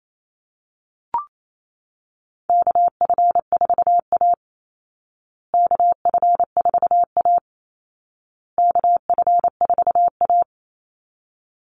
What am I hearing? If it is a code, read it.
Morse code, heard as a keyed tone, about 28 wpm, 700 Hz. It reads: KF4A KF4A KF4A
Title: Call Signs Repeated 2x - 28wpm